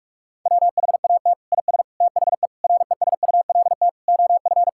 Morse code, heard as a keyed tone, about 46 wpm, 700 Hz. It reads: WHAT IS THE RESULT OF
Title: Sentences from Top 1000 Words - Review Part 5 - 45wpm